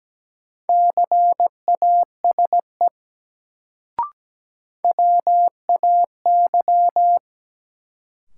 Morse code, heard as a keyed tone, about 17 wpm, 700 Hz. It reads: CASE WAY
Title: Sets of 2 Words - New Word Drill - 17wpm